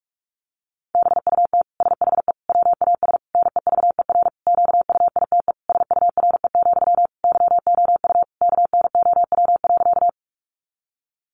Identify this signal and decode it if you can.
Morse code, heard as a keyed tone, about 44 wpm, 700 Hz. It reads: BUT SHE WAS NEVER QUITE SURE, YOU KNOW.
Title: Sentences from Top 700 Words 45wpm